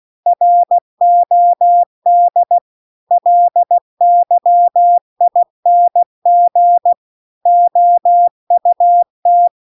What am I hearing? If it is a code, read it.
Morse code, heard as a keyed tone, about 16 wpm, 700 Hz. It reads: ROD LYING OUT
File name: CW-The-Big-Twoheaded-River-2-16WPM